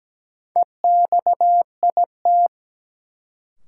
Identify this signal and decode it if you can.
Morse code, heard as a keyed tone, about 17 wpm, 700 Hz. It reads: EXIT
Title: Sentences from Top 1100 Words - New Word Drill - 17wpm